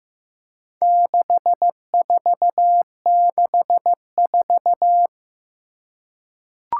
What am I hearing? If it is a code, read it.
Morse code, heard as a keyed tone, about 15 wpm, 700 Hz. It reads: 6464
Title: Four Challenging Letter-Number Combinations 15wpm